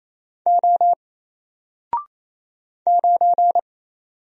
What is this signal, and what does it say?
Morse code, heard as a keyed tone, about 28 wpm, 700 Hz. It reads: O 9
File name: Single-Letter-Number-28wpm